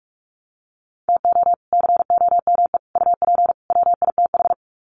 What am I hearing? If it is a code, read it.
Morse code, heard as a keyed tone, about 44 wpm, 700 Hz. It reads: TO COME UP WITH
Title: Sets of 4 Words 45wpm